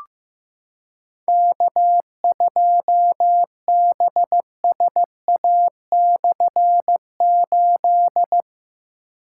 Morse code, heard as a keyed tone, about 16 wpm, 700 Hz. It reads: K2BSA/8
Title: K1USN SST QSO Practice - 15wpm